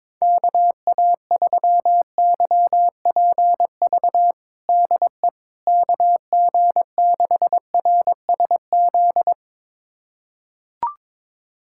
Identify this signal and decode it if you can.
Morse code, heard as a keyed tone, about 22 wpm, 700 Hz. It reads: KA3YPV DE KG6RSZ
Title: USA Callsign DE Callsign 22wpm